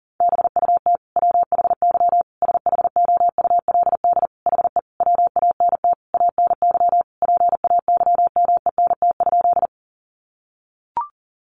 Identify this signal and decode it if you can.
Morse code, heard as a keyed tone, about 37 wpm, 700 Hz. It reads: BUT WHY SHOULD HE WANT ANY PAYMENT?
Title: Sentences from Top 1200 Words 40wpm